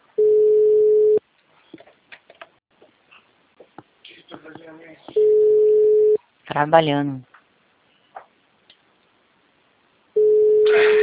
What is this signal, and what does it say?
Ringback tone (Europe, 425 Hz), with speech.